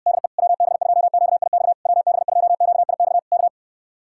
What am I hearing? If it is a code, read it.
Morse code, heard as a keyed tone, about 62 wpm, 700 Hz. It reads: DE WB2QIX WB2QIX K
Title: Speed-Racing Common QSO Phrases - 50wpm